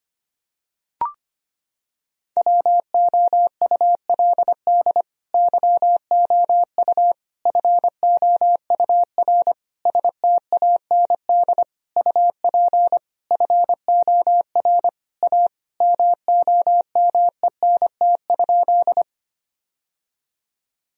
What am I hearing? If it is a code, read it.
Morse code, heard as a keyed tone, about 25 wpm, 700 Hz. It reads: WOULD YOU FOUR STAND UP FOR A MOMENT?
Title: Sentences from Top 1000 Words - Review Part 1 - 25wpm